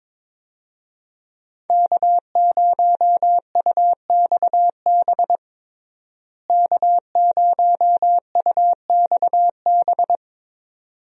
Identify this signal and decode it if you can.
Morse code, heard as a keyed tone, about 21 wpm, 700 Hz. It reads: K0UXB K0UXB
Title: USA Call Signs - 5 Characters - Repeated 2x - 22wpm